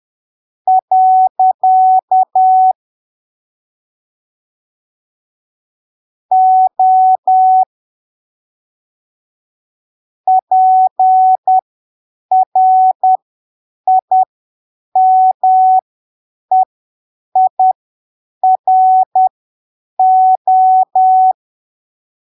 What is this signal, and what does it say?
Morse code, heard as a keyed tone, about 10 wpm, spaced as for 7 wpm, 750 Hz. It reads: . O PRIMEIRO
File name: arquivos / texto / texto 30 10ppm